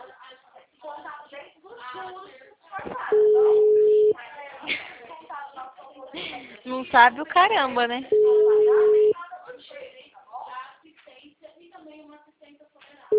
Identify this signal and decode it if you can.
Ringback tone (Europe, 425 Hz), with speech.